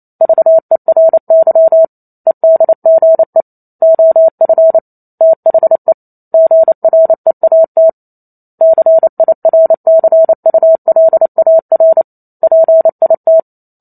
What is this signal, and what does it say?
Morse code, heard as a keyed tone, about 28 wpm, 650 Hz. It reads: VERY EDGE OF THE GREAT CIRCULAR PIT
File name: Chapter 2 - 28 WPM